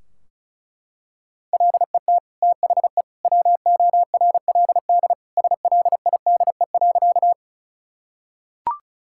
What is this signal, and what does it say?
Morse code, heard as a keyed tone, about 34 wpm, 700 Hz. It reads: LET THE WORLD SLIDE.
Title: Sentences from Top 1600 Words 35wpm